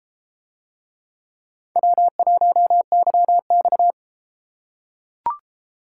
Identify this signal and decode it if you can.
Morse code, heard as a keyed tone, about 32 wpm, 700 Hz. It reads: W1YX